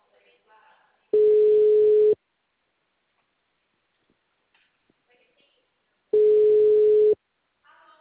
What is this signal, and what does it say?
Ringback tone (Europe, 425 Hz), with speech.
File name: out- -2046- - - .4019